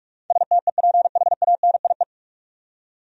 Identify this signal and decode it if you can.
Morse code, heard as a keyed tone, about 47 wpm, 700 Hz. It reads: STEPHANIE
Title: USA Common Names - 45wpm